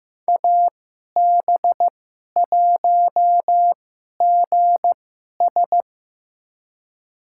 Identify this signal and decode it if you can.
Morse code, heard as a keyed tone, about 15 wpm, spaced as for 11 wpm, 700 Hz. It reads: AB1GS
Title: USA Call Signs - 2x Character Spacing - 15wpm